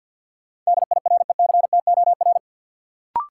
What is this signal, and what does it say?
Morse code, heard as a keyed tone, about 49 wpm, 700 Hz. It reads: DIRECTOR
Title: Sentences from Top 1300 Words - New Word Drill Alt - 50wpm